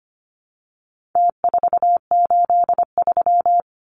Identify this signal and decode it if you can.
Morse code, heard as a keyed tone, about 25 wpm, 700 Hz. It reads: T483
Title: Numbers - Contest Style - 4-Digit 25wpm